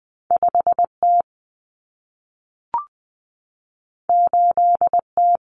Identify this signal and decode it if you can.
Morse code, heard as a keyed tone, about 20 wpm, 700 Hz. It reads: 5T 8T
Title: Common Cut Numbers - 2-Digit 20wpm